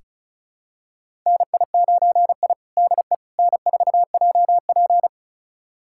Morse code, heard as a keyed tone, about 36 wpm, 700 Hz. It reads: NI9I DE N4JP
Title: USA Callsign DE Callsign - 4 Characters - 35wpm